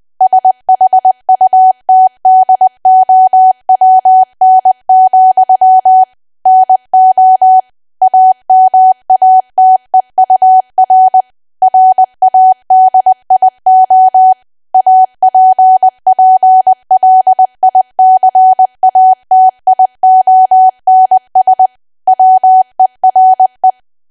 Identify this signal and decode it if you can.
Morse code, heard as a keyed tone, about 20 wpm, 750 Hz. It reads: SHUTDOWN, NO AMATEUR RADIO APPLICATIONS WERE
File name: cwa / W1AW-Level3 / 20WPM